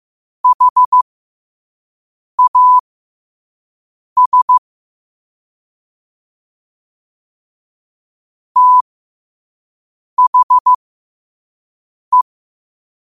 Morse code, heard as a keyed tone, about 15 wpm, spaced as for 5 wpm, 1000 Hz. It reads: HAS THE
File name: qotdfiles / 5wpm